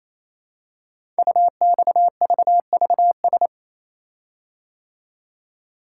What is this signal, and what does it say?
Morse code, heard as a keyed tone, about 27 wpm, 700 Hz. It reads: UXVVS
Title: Code Groups - 5 Letters - 28wpm